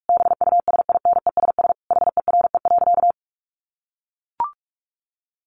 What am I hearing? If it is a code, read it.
Morse code, heard as a keyed tone, about 44 wpm, 700 Hz. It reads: BUSINESS HERE.
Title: Sentences from Top 1300 Words 45wpm